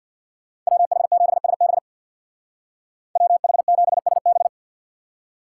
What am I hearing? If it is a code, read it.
Morse code, heard as a keyed tone, about 64 wpm, 700 Hz. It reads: WH7SB WH7SB